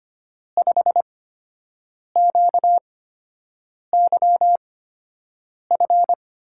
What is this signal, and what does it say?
Morse code, heard as a keyed tone, about 25 wpm, spaced as for 7 wpm, 700 Hz. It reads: 5QYF
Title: USA Call Signs - 6 Characters - 8x Character Spacing - 25wpm